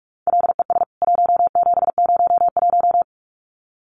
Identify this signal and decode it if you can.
Morse code, heard as a keyed tone, about 44 wpm, 700 Hz. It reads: LES 1701
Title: CWT Exchange 45wpm